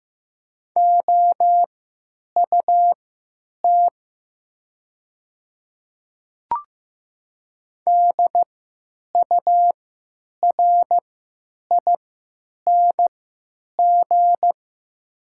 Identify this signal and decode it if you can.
Morse code, heard as a keyed tone, about 15 wpm, spaced as for 9 wpm, 700 Hz. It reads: OUT DURING
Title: 3x Character Spacing - Top 500 Words - 15wpm